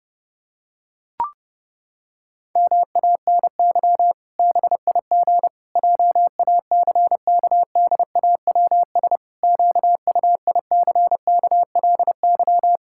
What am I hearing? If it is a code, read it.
Morse code, heard as a keyed tone, about 30 wpm, 700 Hz. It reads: MANY BIG JACKDAWS QUICKLY
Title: Pangram Challenge v2 - 30wpm